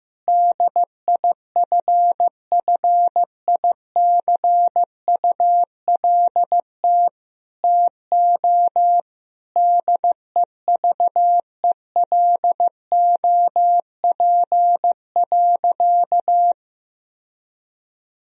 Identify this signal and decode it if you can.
Morse code, heard as a keyed tone, about 15 wpm, 700 Hz. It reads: DIFFICULT TO DEVELOP.
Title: Sentences from Top 1000 Words - Review Part 2 - 15wpm